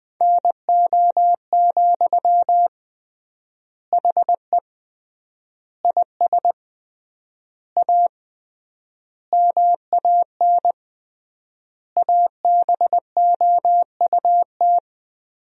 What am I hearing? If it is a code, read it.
Morse code, heard as a keyed tone, about 20 wpm, 700 Hz. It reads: NO, HE IS A MAN ABOUT
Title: 3x Word-Spacing - Sentences from Top 400 Words - 20wpm